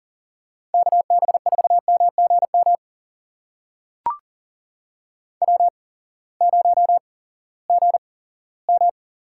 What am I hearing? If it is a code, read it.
Morse code, heard as a keyed tone, about 40 wpm, spaced as for 20 wpm, 700 Hz. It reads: KB4MGM W0GM
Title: USA Call Signs - 8x Character Spacing - 40wpm